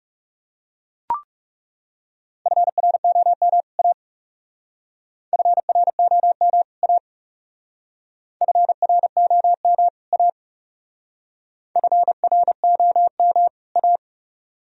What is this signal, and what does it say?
Morse code, heard as a keyed tone, about 38 wpm, 700 Hz. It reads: FROM A FROM A FROM A FROM A